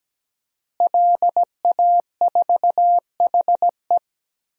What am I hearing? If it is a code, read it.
Morse code, heard as a keyed tone, about 17 wpm, 700 Hz. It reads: LA4HE
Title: DXCC-100 Call Signs - 5 Characters - Repeated 3x - 17wpm